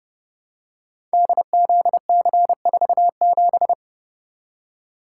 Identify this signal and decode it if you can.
Morse code, heard as a keyed tone, about 29 wpm, 700 Hz. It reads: DZC47
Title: Code Groups - 5 Letter:Numbers - 30wpm